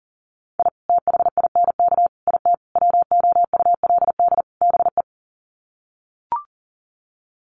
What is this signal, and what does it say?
Morse code, heard as a keyed tone, about 40 wpm, 700 Hz. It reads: I THINK IT WOULD BE
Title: Sets of 5 Words 40wpm